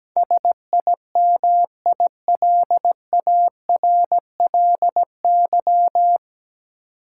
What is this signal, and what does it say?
Morse code, heard as a keyed tone, about 17 wpm, 700 Hz. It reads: SIMILARLY
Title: Sentences from Top 1400 Words - New Word Drill Alt - 17wpm